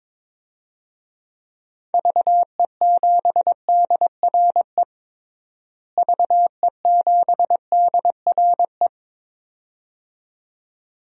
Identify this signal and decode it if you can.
Morse code, heard as a keyed tone, about 22 wpm, 700 Hz. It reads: VE7DRE VE7DRE
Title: DXCC-100 Call Signs - 6 Characters - Repeated 2x - 22wpm